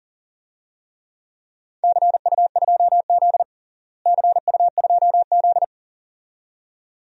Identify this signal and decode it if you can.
Morse code, heard as a keyed tone, about 37 wpm, 700 Hz. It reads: CU2Z CU2Z
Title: DXCC-100 Call Signs - 4 Characters - Repeated 2x - 40wpm